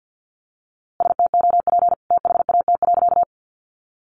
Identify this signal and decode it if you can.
Morse code, heard as a keyed tone, about 50 wpm, 700 Hz. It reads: STOP THAT.
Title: A Sound Of Thunder 50wpm